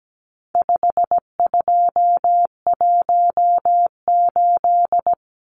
Morse code, heard as a keyed tone, about 17 wpm, 700 Hz. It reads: 5218
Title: Numbers - 4-Digit 17wpm